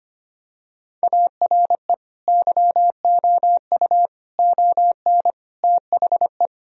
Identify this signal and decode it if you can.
Morse code, heard as a keyed tone, about 25 wpm, 700 Hz. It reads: ARE YOU ON THE
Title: Sentences from Top 1000 Words - Review Part 2 - 25wpm